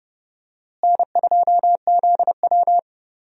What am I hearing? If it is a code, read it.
Morse code, heard as a keyed tone, about 31 wpm, 700 Hz. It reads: N2ZW
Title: USA Call Signs - 4 Characters - Repeated 2x - 30wpm